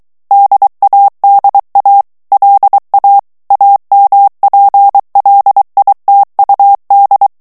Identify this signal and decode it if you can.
Morse code, heard as a keyed tone, about 23 wpm, 800 Hz. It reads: DADA LA AMPLITUD
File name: morse mp3 / ESPAGNOL TEXTE EN CLAIR 2 VITESSE 22 MOTS MINUTES